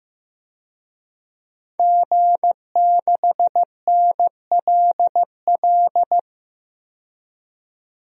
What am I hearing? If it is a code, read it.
Morse code, heard as a keyed tone, about 15 wpm, 700 Hz. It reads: G6NLL
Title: DXCC-100 Call Signs - 5 Characters - 15wpm